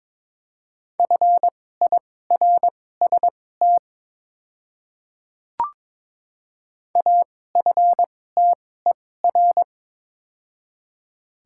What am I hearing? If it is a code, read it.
Morse code, heard as a keyed tone, about 22 wpm, spaced as for 16 wpm, 700 Hz. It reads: FIRST AFTER
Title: 2x Character Spacing - Top 300 Words - 22wpm